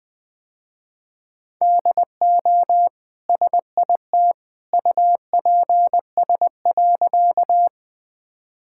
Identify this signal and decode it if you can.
Morse code, heard as a keyed tone, about 20 wpm, 700 Hz. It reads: DO SIT UPS.
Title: Easy Sentences with Hints - 20wpm